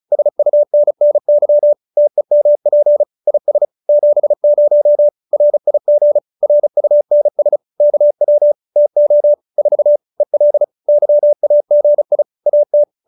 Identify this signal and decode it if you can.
Morse code, heard as a keyed tone, about 34 wpm, 590 Hz. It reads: SUNNY TEMP IS 70 RIG RUNS KW TO 4 EL YAGI AT